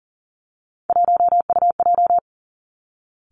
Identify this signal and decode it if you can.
Morse code, heard as a keyed tone, about 38 wpm, 700 Hz. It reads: 1UJ
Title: Three Challenging Letter-Number Combinations 40wpm